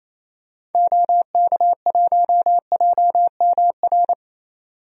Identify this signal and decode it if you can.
Morse code, heard as a keyed tone, about 28 wpm, 700 Hz. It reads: OK1JMR
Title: DXCC-100 Call Signs - 6 Characters - 28wpm